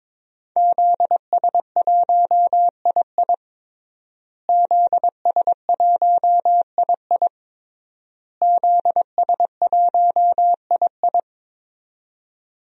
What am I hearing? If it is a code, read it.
Morse code, heard as a keyed tone, about 22 wpm, 700 Hz. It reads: ZS1II ZS1II ZS1II
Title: DXCC-100 Call Signs - 5 Characters - Repeated 3x - 22wpm